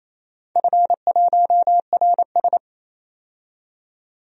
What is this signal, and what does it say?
Morse code, heard as a keyed tone, about 29 wpm, 690 Hz. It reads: F1RS
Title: DXCC-100 Call Signs - 4 Characters - 28wpm